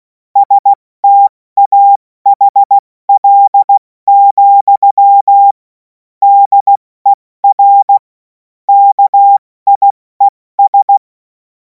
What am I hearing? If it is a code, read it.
Morse code, heard as a keyed tone, about 16 wpm, 800 Hz. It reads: STAHL, DER KIES
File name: Morsekurs / Witze / Witz 10. Februar 2026 70 F80